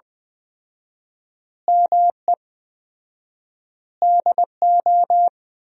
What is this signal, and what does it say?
Morse code, heard as a keyed tone, about 21 wpm, 700 Hz. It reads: ME DO
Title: 4x Word-Spacing - Sentences from Top 300 Words - 20wpm